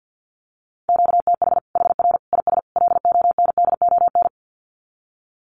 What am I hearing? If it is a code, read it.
Morse code, heard as a keyed tone, about 50 wpm, 700 Hz. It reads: QTH HR IS LONDON
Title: Common QSO Phrases 50wpm